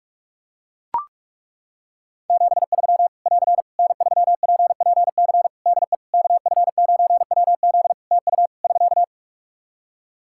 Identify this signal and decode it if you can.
Morse code, heard as a keyed tone, about 41 wpm, 700 Hz. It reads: 73 <AR> N3PPC DE KF9WZ TU <SK>
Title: K1USN SST QSO Practice 45wpm